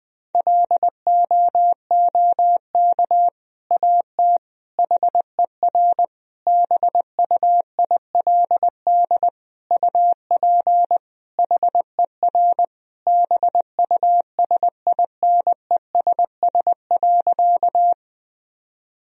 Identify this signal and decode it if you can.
Morse code, heard as a keyed tone, about 20 wpm, 700 Hz. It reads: LOOK AT HER BUILD UP HER BUSINESS.
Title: Sentences from Top 1000 Words - Review Part 1 - 20wpm